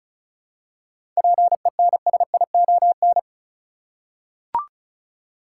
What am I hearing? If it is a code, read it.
Morse code, heard as a keyed tone, about 36 wpm, 700 Hz. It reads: PENSION
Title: Seven Letter Words - 35wpm